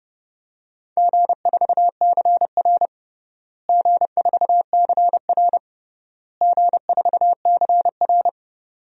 Morse code, heard as a keyed tone, about 30 wpm, 700 Hz. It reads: G4CR G4CR G4CR
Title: DXCC-100 Call Signs - 4 Characters - Repeated 3x - 30wpm